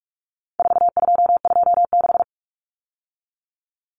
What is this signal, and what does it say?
Morse code, heard as a keyed tone, about 44 wpm, 700 Hz. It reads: 4226
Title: Common Cut Numbers - 4-Digit 45wpm